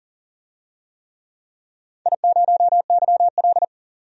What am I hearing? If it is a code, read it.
Morse code, heard as a keyed tone, about 42 wpm, 700 Hz. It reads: I0YL
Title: DXCC-100 Call Signs - 4 Characters - 40wpm